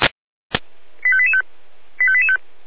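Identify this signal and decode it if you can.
Five-tone selective call (ZVEI), heard as quick runs of five tones, tones of about 70 ms. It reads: 85885 85885
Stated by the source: ZVEI five-tone call received on SDR.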